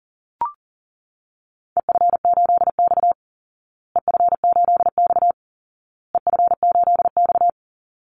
Morse code, heard as a keyed tone, about 40 wpm, 700 Hz. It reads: EF8X EF8X EF8X
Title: Call Signs Repeated 2x - 40wpm